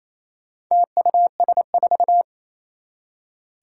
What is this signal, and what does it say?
Morse code, heard as a keyed tone, about 27 wpm, 700 Hz. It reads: TUS4
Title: Four Challenging Letter-Number Combinations 28wpm